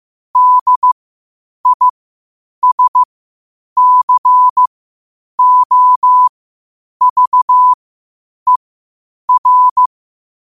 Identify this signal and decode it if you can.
Morse code, heard as a keyed tone, about 15 wpm, spaced as for 8 wpm, 1000 Hz. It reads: DISCOVER
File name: qotdfiles / 8wpm